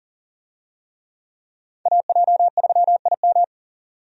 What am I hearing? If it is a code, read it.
Morse code, heard as a keyed tone, about 40 wpm, 700 Hz. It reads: AJ3IM
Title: USA Call Signs - 5 Characters - 40wpm